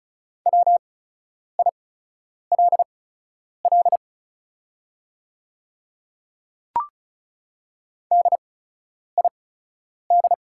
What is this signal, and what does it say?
Morse code, heard as a keyed tone, about 34 wpm, spaced as for 10 wpm, 690 Hz. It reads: WILL DID
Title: 8x Character Spacing - Top 100 Words - 35wpm